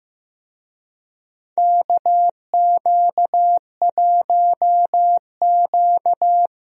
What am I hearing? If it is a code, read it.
Morse code, heard as a keyed tone, about 15 wpm, 700 Hz. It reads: KQ1Q
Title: USA Call Signs - 4 Characters - Repeated 3x - 15wpm